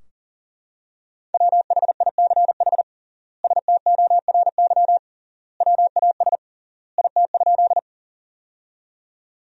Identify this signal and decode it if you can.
Morse code, heard as a keyed tone, about 40 wpm, 700 Hz. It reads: WHICH STORY WAS IT?
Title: 3x Word-Spacing - Sentences from Top 300 Words - 40wpm